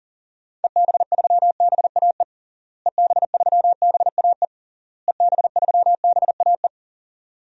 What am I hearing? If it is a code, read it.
Morse code, heard as a keyed tone, about 40 wpm, 700 Hz. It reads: EB3BAE EB3BAE EB3BAE
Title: DXCC-100 Call Signs - 6 Characters - Repeated 3x - 40wpm